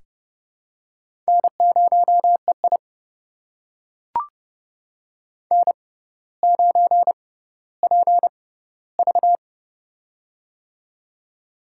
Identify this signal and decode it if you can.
Morse code, heard as a keyed tone, about 30 wpm, spaced as for 15 wpm, 700 Hz. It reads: N0EI N9PV
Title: USA Call Signs - 4 Characters - 6x Character Spacing - 30wpm